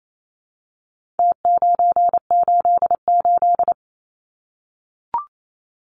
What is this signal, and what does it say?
Morse code, heard as a keyed tone, about 28 wpm, 700 Hz. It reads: T988
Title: Numbers - Contest Style - 4-Digit 28wpm